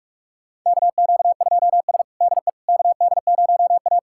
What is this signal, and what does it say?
Morse code, heard as a keyed tone, about 46 wpm, 700 Hz. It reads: KQ2S DE KD0A
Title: USA Callsign DE Callsign - 4 Characters - 45wpm